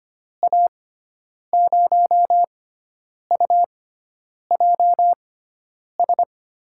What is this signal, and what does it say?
Morse code, heard as a keyed tone, about 25 wpm, spaced as for 9 wpm, 700 Hz. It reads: A0UJS
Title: USA Call Signs - 6 Characters - 6x Character Spacing - 25wpm